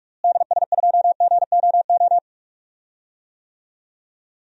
Morse code, heard as a keyed tone, about 47 wpm, 700 Hz. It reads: DS2GOO
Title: DXCC-100 Call Signs - 6 Characters - Repeated 3x - 45wpm